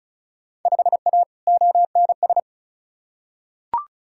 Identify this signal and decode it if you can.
Morse code, heard as a keyed tone, about 34 wpm, 700 Hz. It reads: 5A ONS
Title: Field Day Exchanges Encore - 35wpm